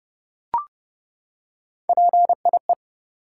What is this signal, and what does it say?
Morse code, heard as a keyed tone, about 30 wpm, 700 Hz. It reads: PIE
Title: Three Letter Words - 30wpm